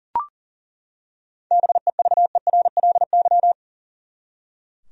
Morse code, heard as a keyed tone, about 40 wpm, 700 Hz. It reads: BEVERLY
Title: USA Common Names - 40wpm